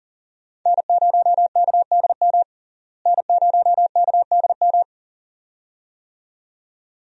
Morse code, heard as a keyed tone, about 37 wpm, 700 Hz. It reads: N0KDM N0KDM
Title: USA Call Signs - 5 Characters - Repeated 2x - 40wpm